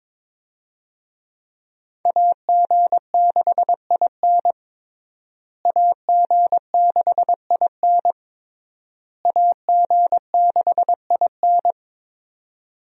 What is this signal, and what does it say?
Morse code, heard as a keyed tone, about 22 wpm, 700 Hz. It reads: AG6IN AG6IN AG6IN
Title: USA Call Signs - 5 Characters - Repeated 3x - 22wpm